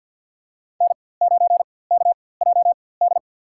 Morse code, heard as a keyed tone, about 53 wpm, spaced as for 24 wpm, 690 Hz. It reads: N9KJD
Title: USA Call Signs - 4x Character Spacing - 50wpm